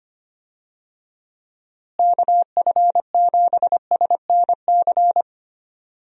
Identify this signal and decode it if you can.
Morse code, heard as a keyed tone, about 25 wpm, 700 Hz. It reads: KF7SNC
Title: USA Call Signs - 6 Characters - Repeated 2x - 25wpm